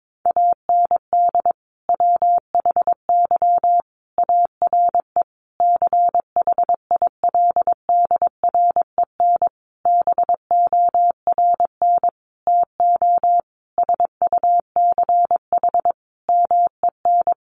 Morse code, heard as a keyed tone, about 22 wpm, 700 Hz. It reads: AND WHY ARE CHILDREN BORN TO SUCH MEN
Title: Sentences from Top 1000 Words 22wpm